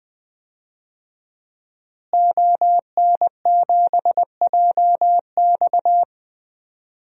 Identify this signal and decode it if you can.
Morse code, heard as a keyed tone, about 20 wpm, 700 Hz. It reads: ON7JX
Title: DXCC-100 Call Signs - 5 Characters - Repeated 3x - 20wpm